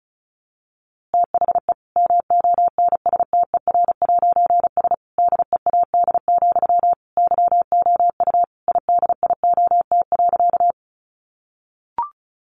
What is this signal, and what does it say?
Morse code, heard as a keyed tone, about 34 wpm, 700 Hz. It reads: THE MONSTER'S DEAD, YOU IDIOT.
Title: A Sound Of Thunder 35wpm